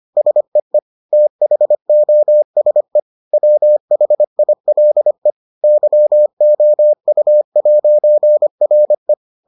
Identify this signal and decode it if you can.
Morse code, heard as a keyed tone, about 25 wpm, 600 Hz. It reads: SEE THOSE WHILE YOU'RE